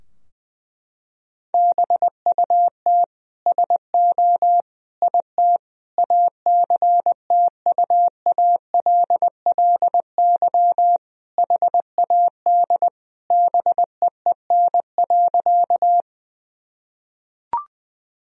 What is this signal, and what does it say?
Morse code, heard as a keyed tone, about 20 wpm, 700 Hz. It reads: BUT SO IT ACTUALLY HAD BEEN.
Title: Sentences from Top 1000 Words - Review Part 4 - 20wpm